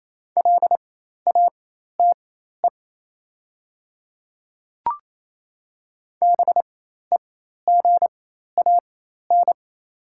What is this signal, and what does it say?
Morse code, heard as a keyed tone, about 28 wpm, spaced as for 13 wpm, 700 Hz. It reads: LATE BEGAN
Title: 4x Character Spacing - Top 300 Words - 28wpm